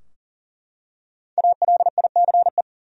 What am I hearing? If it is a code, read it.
Morse code, heard as a keyed tone, about 40 wpm, 700 Hz. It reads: ALICE
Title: USA Common Names - Max 5 Letters - 40wpm